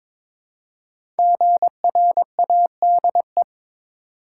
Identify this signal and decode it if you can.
Morse code, heard as a keyed tone, about 22 wpm, 700 Hz. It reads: GRADE
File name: 5 Letter Words - First 3 Letters Given - 22wpm